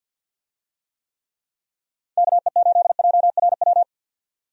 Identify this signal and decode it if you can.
Morse code, heard as a keyed tone, about 50 wpm, 700 Hz. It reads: KE8JRW
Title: USA Call Signs - 6 Characters - 50wpm